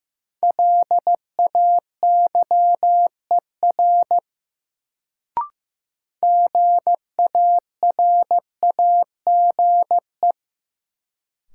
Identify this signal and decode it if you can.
Morse code, heard as a keyed tone, about 15 wpm, 700 Hz. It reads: LAYER GARAGE
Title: Sentences from Top 1600 Words - New Word Drill - 15wpm